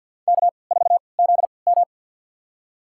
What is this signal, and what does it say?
Morse code, heard as a keyed tone, about 51 wpm, spaced as for 28 wpm, 700 Hz. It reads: K4ZM
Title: USA Call Signs - 3x Character Spacing - 50wpm